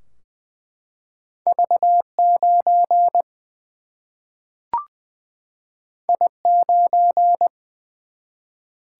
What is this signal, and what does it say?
Morse code, heard as a keyed tone, about 20 wpm, 700 Hz. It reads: V 9 I 9
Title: Single-Letter Single-Number 20wpm